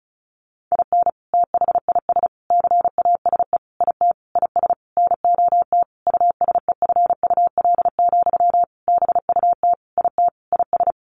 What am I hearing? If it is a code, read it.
Morse code, heard as a keyed tone, about 34 wpm, 700 Hz. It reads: IN THIS CASE IT IS NOT USEFUL, BUT IT IS
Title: Sentences from Top 1400 Words 35wpm